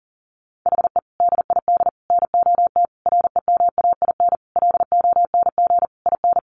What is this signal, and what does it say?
Morse code, heard as a keyed tone, about 40 wpm, 700 Hz. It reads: HE DID NOT REMAIN LONG IN
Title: Sentences from Top 1500 Words 40wpm